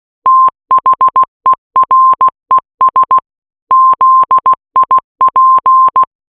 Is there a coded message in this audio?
Yes: Morse code from 0.3 to 6.0 s, heard as a keyed tone, about 16 wpm, 1050 Hz. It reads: THERES ZIP